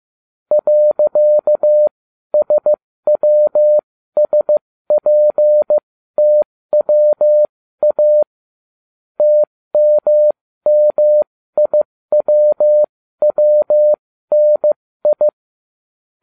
Morse code, heard as a keyed tone, about 15 wpm, spaced as for 13 wpm, 600 Hz. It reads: .SWSPTWA TMMIWWNI